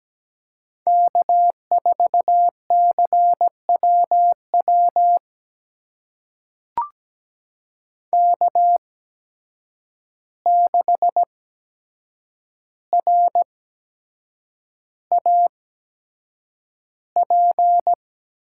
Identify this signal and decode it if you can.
Morse code, heard as a keyed tone, about 17 wpm, spaced as for 7 wpm, 700 Hz. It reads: K4CWW K6RAP
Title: USA Call Signs - 5 Characters - 8x Character Spacing - 17wpm